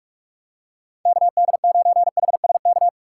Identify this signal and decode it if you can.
Morse code, heard as a keyed tone, about 41 wpm, 700 Hz. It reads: KD9HSK